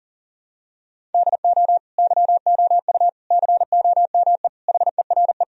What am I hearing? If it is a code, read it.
Morse code, heard as a keyed tone, about 40 wpm, 700 Hz. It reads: DO YOU COME HERE
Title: Sentences from Top 1000 Words - Review Part 1 - 40wpm